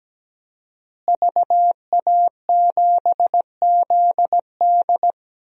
Morse code, heard as a keyed tone, about 17 wpm, 700 Hz. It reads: VA7ZD
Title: DXCC-100 Call Signs - 5 Characters - Repeated 3x - 17wpm